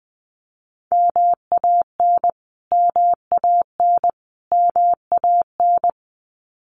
Morse code, heard as a keyed tone, about 20 wpm, 700 Hz. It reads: MAN MAN MAN
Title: Top 500 Words Repeated 20wpm